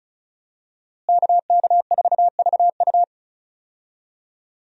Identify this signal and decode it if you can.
Morse code, heard as a keyed tone, about 36 wpm, 700 Hz. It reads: KK4VU
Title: USA Call Signs - 5 Characters - Repeated 2x - 35wpm